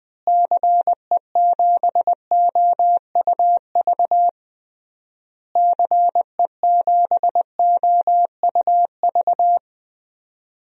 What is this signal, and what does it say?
Morse code, heard as a keyed tone, about 21 wpm, 700 Hz. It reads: CE7OUV CE7OUV
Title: DXCC-100 Call Signs - 6 Characters - Repeated 3x - 20wpm